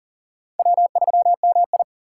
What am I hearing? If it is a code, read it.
Morse code, heard as a keyed tone, about 40 wpm, 700 Hz. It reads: W3MI
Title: USA Call Signs - 2x Character Spacing - 40wpm